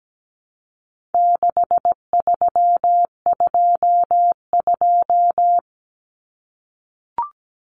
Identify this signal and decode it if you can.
Morse code, heard as a keyed tone, about 17 wpm, 700 Hz. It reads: 6322
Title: Numbers - 4-Digit 17wpm